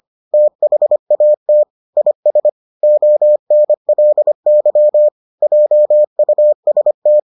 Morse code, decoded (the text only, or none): THAT IS ONLY JUST